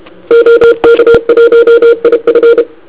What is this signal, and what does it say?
Morse code, heard as a keyed tone, about 31 wpm, 470 Hz. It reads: OK1IF
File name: ARCHIV / KX3 / Sekana znackaCW